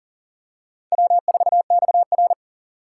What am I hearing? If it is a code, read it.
Morse code, heard as a keyed tone, about 40 wpm, 700 Hz. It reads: W4XR
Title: USA Call Signs - 4 Characters - 2x Character Spacing - 40wpm